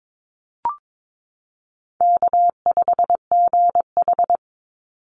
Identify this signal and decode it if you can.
Morse code, heard as a keyed tone, about 22 wpm, 700 Hz. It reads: K5GH
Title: Easy Call Signs - 22wpm